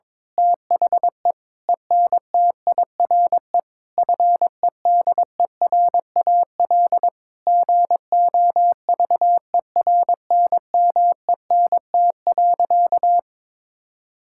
Morse code, heard as a keyed tone, about 22 wpm, 700 Hz. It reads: THE ENTIRE FEDERAL GOVERNMENT.
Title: Sentences from Top 1000 Words - Review Part 5 - 22wpm